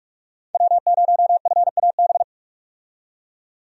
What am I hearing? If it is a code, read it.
Morse code, heard as a keyed tone, about 47 wpm, 700 Hz. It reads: W0FAB
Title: USA Call Signs - 5 Characters - 45wpm